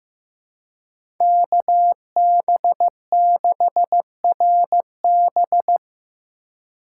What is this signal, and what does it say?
Morse code, heard as a keyed tone, about 15 wpm, 700 Hz. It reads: KB6RB
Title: USA Call Signs - 5 Characters - 2x Character Spacing - 15wpm